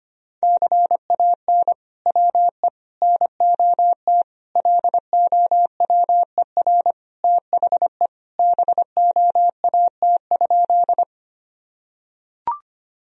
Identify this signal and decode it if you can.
Morse code, heard as a keyed tone, about 25 wpm, 700 Hz. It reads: CAN WE NOT LOWER THE BOAT?
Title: Sentences from Top 1000 Words - Review Part 4 - 25wpm